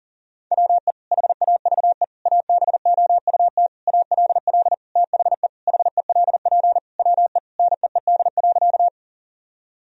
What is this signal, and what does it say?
Morse code, heard as a keyed tone, about 40 wpm, 700 Hz. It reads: WE HAVE ABOUT ALL THE HELP WE NEED.
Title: Sentences from Top 1000 Words - Review Part 1 - 40wpm